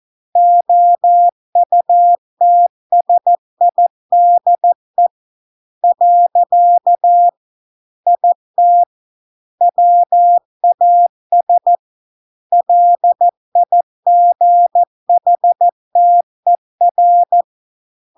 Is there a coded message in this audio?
Yes: Morse code from 0.4 to 17.4 s, heard as a keyed tone, about 14 wpm, 700 Hz. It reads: OUTSIDE . IT WAS LIGHTER